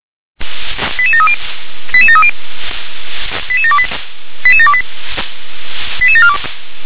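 Five-tone selective call (ZVEI), heard as quick runs of five tones, tones of about 70 ms. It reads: 88730 88730 88728 88728 88742
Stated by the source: ZVEI five-tone selective call received on an SDR.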